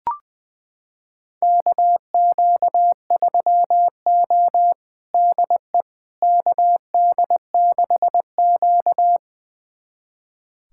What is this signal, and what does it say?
Morse code, heard as a keyed tone, about 20 wpm, 700 Hz. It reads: KQ3O DE KD6Q
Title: USA Callsign DE Callsign - 4 Characters - 20wpm